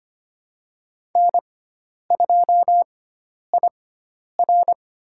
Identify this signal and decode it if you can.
Morse code, heard as a keyed tone, about 25 wpm, spaced as for 10 wpm, 700 Hz. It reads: N2IR
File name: Lesson 16g - Letter c - Call Signs 25wpm